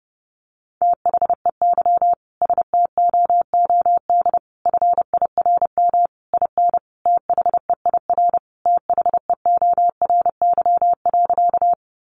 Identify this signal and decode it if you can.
Morse code, heard as a keyed tone, about 30 wpm, 700 Hz. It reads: THEY STOOD FIRM IN THEIR THEORY.
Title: Sentences from Top 1400 Words 30wpm